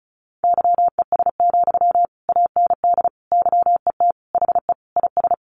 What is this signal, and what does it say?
Morse code, heard as a keyed tone, about 34 wpm, 700 Hz. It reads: YES, AND YET HE IS
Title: Sentences from Top 700 Words 35wpm